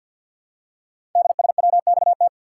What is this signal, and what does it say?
Morse code, heard as a keyed tone, about 52 wpm, 700 Hz. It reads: DSWXT
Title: Code Groups - 5 Letters - 50wpm